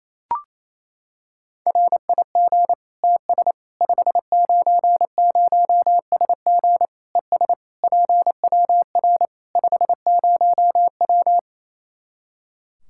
Morse code, heard as a keyed tone, about 27 wpm, 700 Hz. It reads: RIG TS 590SG ES PWR 50W
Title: Practice QSO - Exchange 2 - Info - 28wpm